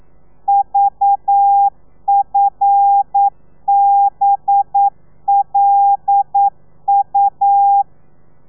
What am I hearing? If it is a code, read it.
Morse code, heard as a keyed tone, about 9 wpm, 800 Hz. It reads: VFBLU
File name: telegrafia / i2tfj / Lez5 40